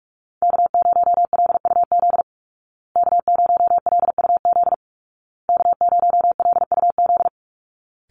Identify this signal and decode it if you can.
Morse code, heard as a keyed tone, about 44 wpm, 700 Hz. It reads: K0LUZ K0LUZ K0LUZ
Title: Call Signs Repeated 2x - 45wpm